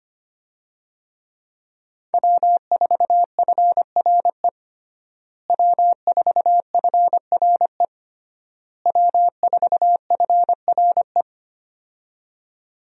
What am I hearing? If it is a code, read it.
Morse code, heard as a keyed tone, about 25 wpm, 700 Hz. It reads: W4FRE W4FRE W4FRE
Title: USA Call Signs - 5 Characters - Repeated 3x - 25wpm